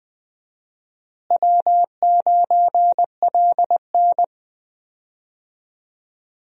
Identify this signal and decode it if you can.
Morse code, heard as a keyed tone, about 21 wpm, 700 Hz. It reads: W9LN
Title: USA Call Signs - 4 Characters - Repeated 3x - 20wpm